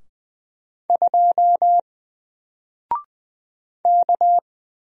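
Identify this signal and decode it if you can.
Morse code, heard as a keyed tone, about 20 wpm, 700 Hz. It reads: 2 K
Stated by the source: Single letter and number Morse drill at 20 wpm, from N4LCD.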